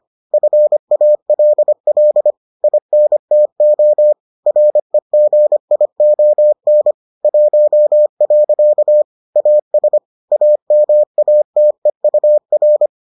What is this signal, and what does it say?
Morse code, heard as a keyed tone, about 25 wpm, 600 Hz. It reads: FALL INTO REGION 1. AS AMATEUR